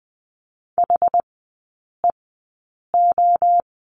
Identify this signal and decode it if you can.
Morse code, heard as a keyed tone, about 20 wpm, spaced as for 8 wpm, 700 Hz. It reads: HEO
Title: 060 Lesson 11e - Letter h - Three Characters 20wpm